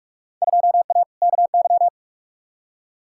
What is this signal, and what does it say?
Morse code, heard as a keyed tone, about 41 wpm, 700 Hz. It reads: 2A KY
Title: Field Day Exchanges - 45wpm